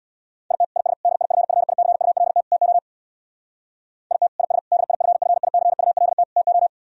Morse code, heard as a keyed tone, about 61 wpm, 700 Hz. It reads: IT IS DIFFICULT TO IT IS DIFFICULT TO
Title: Speed-Racing - Sets of 4 Words 50wpm